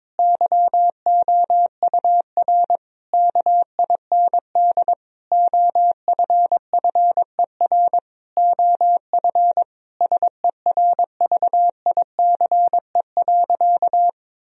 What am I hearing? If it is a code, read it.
Morse code, heard as a keyed tone, about 22 wpm, 700 Hz. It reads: YOUR KIND OFFER OF SERVICE.
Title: Sentences from Top 1000 Words - Review Part 4 - 22wpm